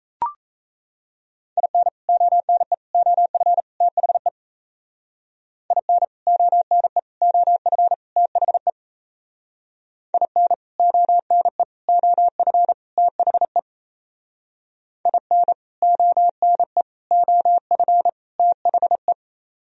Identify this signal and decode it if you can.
Morse code, heard as a keyed tone, about 37 wpm, 700 Hz. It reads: IN ONE OF THE IN ONE OF THE IN ONE OF THE IN ONE OF THE